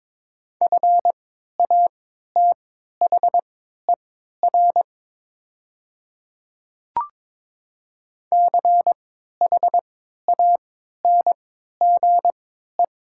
Morse code, heard as a keyed tone, about 22 wpm, spaced as for 12 wpm, 700 Hz. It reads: FATHER CHANGE
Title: 3x Character Spacing - Top 200 Words - 22wpm